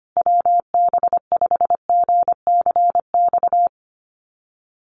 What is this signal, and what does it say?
Morse code, heard as a keyed tone, about 25 wpm, 700 Hz. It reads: WB5GCX
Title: USA Call Signs Repeated 2x 25wpm